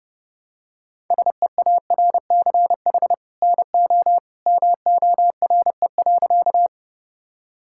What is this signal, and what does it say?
Morse code, heard as a keyed tone, about 30 wpm, 700 Hz. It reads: SEARCH NO MORE.
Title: Sentences from Top 1000 Words - Review Part 5 - 30wpm